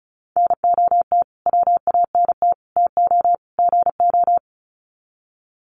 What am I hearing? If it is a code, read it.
Morse code, heard as a keyed tone, about 34 wpm, 700 Hz. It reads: NOT WANT TO GO
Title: Sets of 4 Words 35wpm